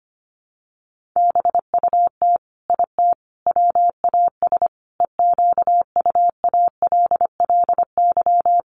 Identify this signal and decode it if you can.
Morse code, heard as a keyed tone, about 25 wpm, 700 Hz. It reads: BUT IT WAS EQUALLY
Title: Sentences from Top 1400 Words 25wpm